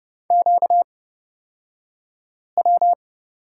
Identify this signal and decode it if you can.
Morse code, heard as a keyed tone, about 30 wpm, spaced as for 5 wpm, 700 Hz. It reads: QW
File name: course / 30:10wpm / 235 Lesson 38d - Letter q - Two Characters 30wpm